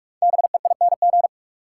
Morse code, heard as a keyed tone, about 45 wpm, 700 Hz. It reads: BEING
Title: Five Letter Words - 45wpm